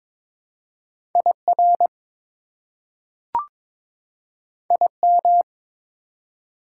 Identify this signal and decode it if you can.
Morse code, heard as a keyed tone, about 22 wpm, 700 Hz. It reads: IR IM
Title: Prefixes and Suffixes 22wpm